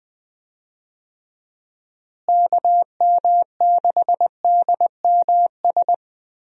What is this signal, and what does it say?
Morse code, heard as a keyed tone, about 20 wpm, 700 Hz. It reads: KM6DMS
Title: USA Call Signs - 6 Characters - Repeated 3x - 20wpm